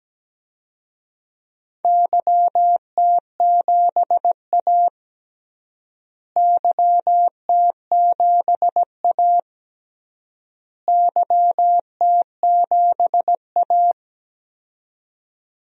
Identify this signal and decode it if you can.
Morse code, heard as a keyed tone, about 17 wpm, 700 Hz. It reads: YT7A YT7A YT7A
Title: DXCC-100 Call Signs - 4 Characters - Repeated 3x - 17wpm